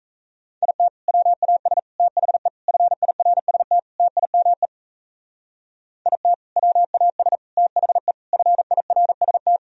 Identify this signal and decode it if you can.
Morse code, heard as a keyed tone, about 34 wpm, 700 Hz. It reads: IT WAS THE FIRST TIME IT WAS THE FIRST
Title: Speed-Racing - Sets of 5 Words 28wpm